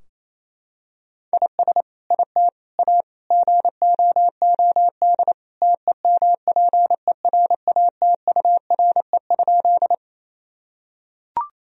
Morse code, heard as a keyed tone, about 28 wpm, 700 Hz. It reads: IS IT A GOOD TEMPERATURE?
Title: Sentences from Top 1000 Words - Review Part 4 - 28wpm